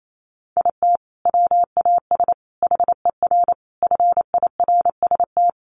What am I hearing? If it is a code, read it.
Morse code, heard as a keyed tone, about 28 wpm, 700 Hz. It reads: IT WAS HER FIRST
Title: Sentences from Top 1100 Words 28wpm